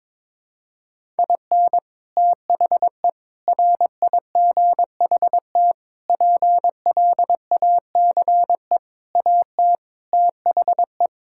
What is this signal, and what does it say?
Morse code, heard as a keyed tone, about 22 wpm, 700 Hz. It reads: IN THE RIGHT PLACE AT THE
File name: practice / 22wpm / Sets of 8 Words - 22wpm